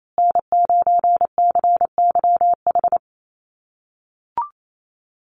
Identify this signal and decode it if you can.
Morse code, heard as a keyed tone, about 28 wpm, 690 Hz. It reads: N9CYH
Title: USA Call Signs Repeated 3x 28wpm